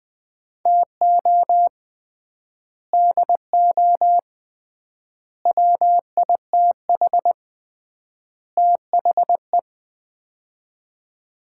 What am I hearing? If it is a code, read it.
Morse code, heard as a keyed tone, about 21 wpm, 700 Hz. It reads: TO DO WITH THE
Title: 3x Word-Spacing - Sets of 4 Words - 20wpm